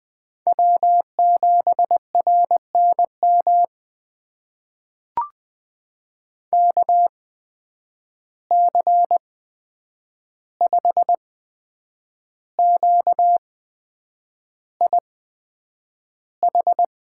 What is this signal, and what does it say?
Morse code, heard as a keyed tone, about 20 wpm, spaced as for 8 wpm, 700 Hz. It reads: W7RNM KC5QIH
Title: USA Call Signs - 8x Character Spacing - 20wpm